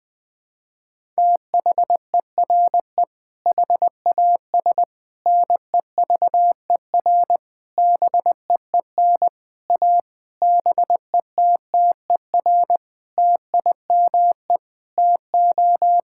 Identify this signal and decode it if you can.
Morse code, heard as a keyed tone, about 20 wpm, 700 Hz. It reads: THERE HAS NEVER BEEN A BETTER TIME TO
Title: Sentences with 7-Word Phrases - Spoken Beginning - 20wpm